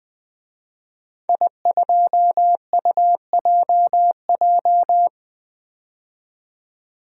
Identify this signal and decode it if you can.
Morse code, heard as a keyed tone, about 20 wpm, 700 Hz. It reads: I2UJJ
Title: DXCC-100 Call Signs - 5 Characters - 20wpm